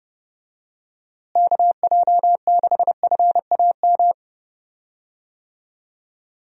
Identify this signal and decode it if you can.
Morse code, heard as a keyed tone, about 30 wpm, 700 Hz. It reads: KJ6FAM
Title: USA Call Signs - 6 Characters - Repeated 2x - 30wpm